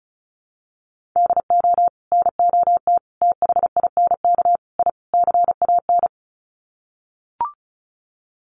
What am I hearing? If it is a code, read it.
Morse code, heard as a keyed tone, about 34 wpm, 700 Hz. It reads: DO NOT THINK I CAN
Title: Sets of 5 Words Encore2 35wpm